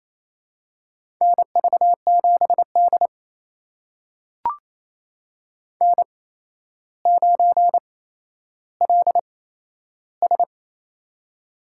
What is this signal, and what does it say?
Morse code, heard as a keyed tone, about 27 wpm, spaced as for 12 wpm, 700 Hz. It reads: NV7D N9LS
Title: USA Call Signs - 4 Characters - 8x Character Spacing - 28wpm